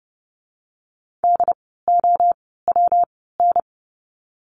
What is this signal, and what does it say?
Morse code, heard as a keyed tone, about 30 wpm, 700 Hz. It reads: D O W N
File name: course / 30:10wpm / 115 Lesson 19f - Letter w - Words 30wpm